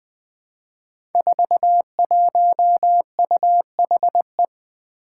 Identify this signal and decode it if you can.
Morse code, heard as a keyed tone, about 20 wpm, 700 Hz. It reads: 41UHE
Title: Five Challenging Letter-Number Combinations 20wpm